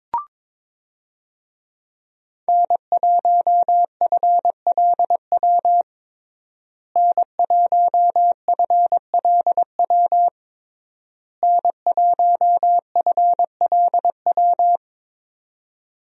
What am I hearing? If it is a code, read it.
Morse code, heard as a keyed tone, about 22 wpm, 700 Hz. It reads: N1FLW N1FLW N1FLW
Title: USA Call Signs - 5 Characters - Repeated 3x - 22wpm